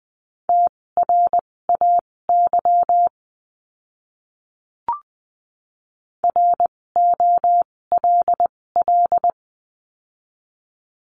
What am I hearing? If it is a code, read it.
Morse code, heard as a keyed tone, about 20 wpm, spaced as for 16 wpm, 700 Hz. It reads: TRAY ROLL
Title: 256 Lesson 42a - Words 12-20wpm